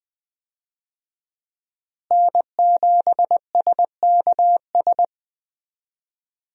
Morse code, heard as a keyed tone, about 20 wpm, 700 Hz. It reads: N7SKS
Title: USA Call Signs - 5 Characters - Repeated 2x - 20wpm